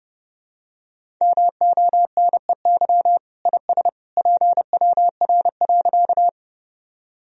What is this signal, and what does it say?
Morse code, heard as a keyed tone, about 30 wpm, 700 Hz. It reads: MONEY IS PWR.
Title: Sentences from Top 500 Words and Abbreviations 30wpm